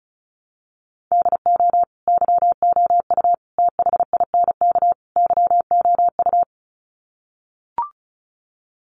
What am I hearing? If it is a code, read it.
Morse code, heard as a keyed tone, about 34 wpm, 700 Hz. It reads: DO YOU THINK YOU
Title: Sets of 4 Words 35wpm